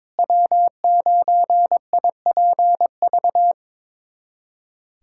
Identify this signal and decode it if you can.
Morse code, heard as a keyed tone, about 22 wpm, 700 Hz. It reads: W9IPV
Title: USA Callsign DE Callsign - 5 Characters - 22wpm